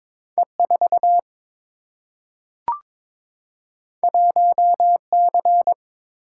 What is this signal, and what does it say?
Morse code, heard as a keyed tone, about 22 wpm, 700 Hz. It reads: E 4 1 C
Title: Single-Letter Single-Number 22wpm